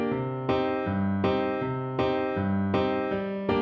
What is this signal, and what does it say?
There is no code.